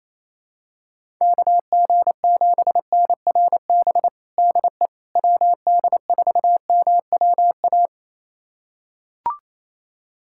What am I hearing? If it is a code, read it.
Morse code, heard as a keyed tone, about 28 wpm, 700 Hz. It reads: KG7NRB DE WD4MWA
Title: USA Callsign DE Callsign 28wpm